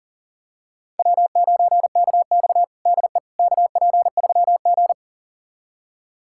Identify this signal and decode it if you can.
Morse code, heard as a keyed tone, about 40 wpm, 700 Hz. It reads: W9KX DE KP3G
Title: USA Callsign DE Callsign - 4 Characters - 40wpm